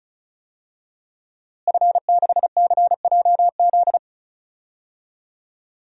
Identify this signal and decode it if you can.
Morse code, heard as a keyed tone, about 36 wpm, 700 Hz. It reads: F6CJZ
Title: DXCC-100 Call Signs - 5 Characters - Repeated 3x - 35wpm